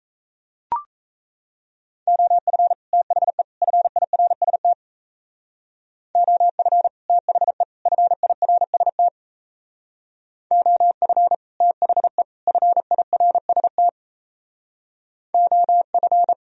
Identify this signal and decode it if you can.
Morse code, heard as a keyed tone, about 34 wpm, 700 Hz. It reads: OF THE FIRST OF THE FIRST OF THE FIRST OF
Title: Speed-Racing - Sets of 3 Words 28wpm